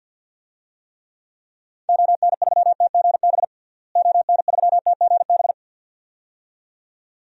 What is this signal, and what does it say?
Morse code, heard as a keyed tone, about 52 wpm, 700 Hz. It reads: ON3TGB ON3TGB
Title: DXCC-100 Call Signs - 6 Characters - Repeated 2x - 50wpm